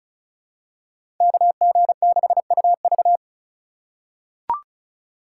Morse code, heard as a keyed tone, about 36 wpm, 700 Hz. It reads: KG6UV
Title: USA Call Signs - 5 Characters - 2x Character Spacing - 35wpm